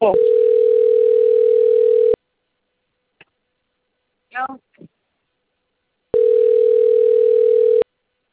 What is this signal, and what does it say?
Ringback tone (North America), 440 + 480 Hz, with speech.
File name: out- - - - - .42268